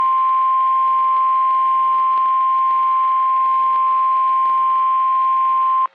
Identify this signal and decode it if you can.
Emergency Alert System (SAME) tones (the 1050 Hz attention tone). It is heard from 0.8 to 5.2 s.
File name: eas messages / EAS WXM31 Elmira